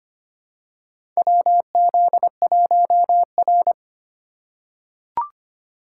Morse code, heard as a keyed tone, about 25 wpm, 700 Hz. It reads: WZ1R
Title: USA Call Signs - 4 Characters - 2x Character Spacing - 25wpm